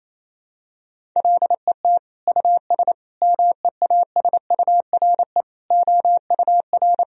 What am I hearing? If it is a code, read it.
Morse code, heard as a keyed tone, about 28 wpm, 700 Hz. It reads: LET US MEASURE OUR
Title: Sentences from Top 1000 Words - Review Part 2 - 28wpm